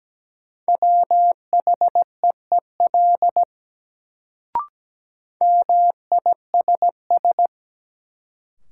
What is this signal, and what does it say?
Morse code, heard as a keyed tone, about 17 wpm, 700 Hz. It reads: WHEEL MISS
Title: Sentences from Top 500 Words - New Word Drill - 17wpm